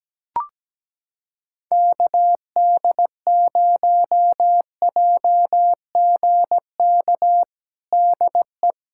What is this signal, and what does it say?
Morse code, heard as a keyed tone, about 17 wpm, 700 Hz. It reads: KD0JGK DE
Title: USA Callsign DE Callsign - 6 Characters - 17wpm